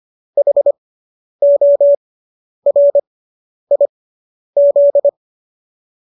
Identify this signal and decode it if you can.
Morse code, heard as a keyed tone, about 25 wpm, spaced as for 10 wpm, 570 Hz. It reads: HORIZ